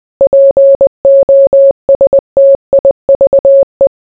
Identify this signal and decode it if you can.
Morse code, heard as a keyed tone, about 20 wpm, 550 Hz. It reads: POSTIVE